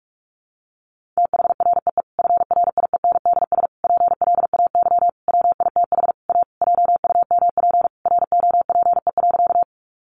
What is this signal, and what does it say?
Morse code, heard as a keyed tone, about 44 wpm, 700 Hz. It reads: THREE FRIENDS PLAY WITH A JUMP ROPE.
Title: Easy Phrases 45wpm